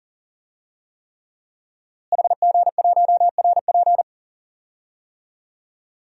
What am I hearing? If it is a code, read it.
Morse code, heard as a keyed tone, about 39 wpm, 700 Hz. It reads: HG1RP
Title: DXCC-100 Call Signs - 5 Characters - Repeated 3x - 40wpm